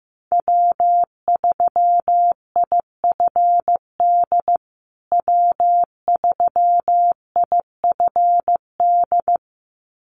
Morse code, heard as a keyed tone, about 15 wpm, 700 Hz. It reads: W3IFD W3IFD
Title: USA Call Signs Repeated 2x 15wpm